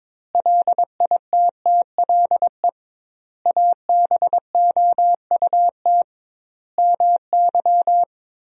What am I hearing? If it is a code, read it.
Morse code, heard as a keyed tone, about 22 wpm, 700 Hz. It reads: LITTLE ABOUT MY
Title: 2x Word-Spacing - Sentences from Top 400 Words - 22wpm